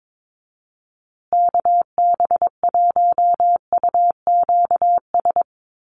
Morse code, heard as a keyed tone, about 22 wpm, 700 Hz. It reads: KB1UQS
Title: USA Call Signs Repeated 3x 22wpm